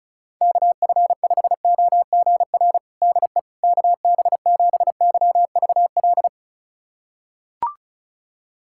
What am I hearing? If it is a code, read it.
Morse code, heard as a keyed tone, about 36 wpm, 700 Hz. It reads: KF5OGR DE KB7YVL
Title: USA Callsign DE Callsign - 6 Characters - 35wpm